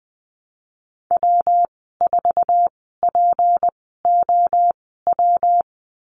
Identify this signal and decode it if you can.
Morse code, heard as a keyed tone, about 20 wpm, spaced as for 15 wpm, 700 Hz. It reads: W4POW
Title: 179 Lesson 29g - Slash - Call Signs 20wpm